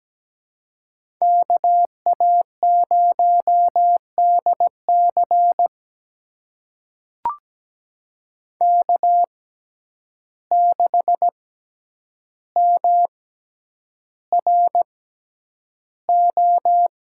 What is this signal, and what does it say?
Morse code, heard as a keyed tone, about 17 wpm, spaced as for 9 wpm, 700 Hz. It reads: KA0DC K6MRO
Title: USA Call Signs - 5 Characters - 6x Character Spacing - 17wpm